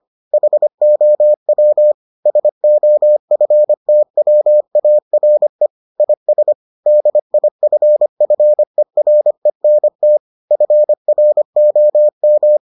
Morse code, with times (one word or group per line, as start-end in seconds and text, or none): 0.3-1.9 HOW
2.3-5.7 SOFTWARE
6.0-6.5 IS
6.9-10.2 DIFFERENT
10.5-12.6 FROM